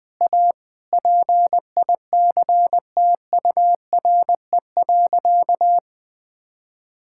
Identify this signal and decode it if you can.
Morse code, heard as a keyed tone, about 20 wpm, 700 Hz. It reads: A PICTURE.
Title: Sentences with 8-Word Phrases - Spoken Beginning - 20wpm